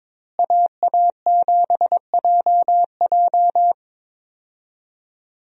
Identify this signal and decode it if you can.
Morse code, heard as a keyed tone, about 23 wpm, 700 Hz. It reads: AA7JJ
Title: USA Call Signs - 5 Characters - Repeated 3x - 22wpm